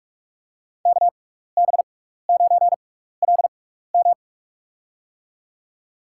Morse code, heard as a keyed tone, about 44 wpm, spaced as for 15 wpm, 700 Hz. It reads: KB9LM
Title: USA Call Signs - 5 Characters - 6x Character Spacing - 45wpm